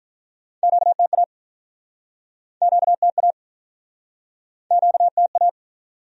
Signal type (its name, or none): Morse code